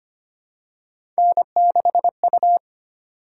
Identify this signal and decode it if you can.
Morse code, heard as a keyed tone, about 25 wpm, 700 Hz. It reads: N6U
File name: Three Letter-Number Combinations - 25wpm